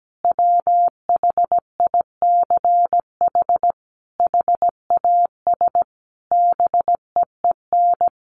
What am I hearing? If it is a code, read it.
Morse code, heard as a keyed tone, about 17 wpm, 700 Hz. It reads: WHICH HAS BEEN
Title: Sentences from Top 1400 Words 17wpm